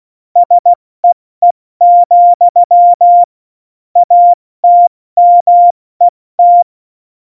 Morse code, heard as a keyed tone, about 16 wpm, 700 Hz. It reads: SEE, ATMET
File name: Morsekurs / Witze / Witz 08. Dezember 2020, 70 F80